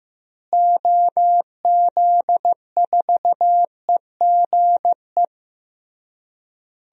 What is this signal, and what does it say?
Morse code, heard as a keyed tone, about 15 wpm, 700 Hz. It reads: OZ4EGE
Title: DXCC-100 Call Signs - 6 Characters - Repeated 3x - 15wpm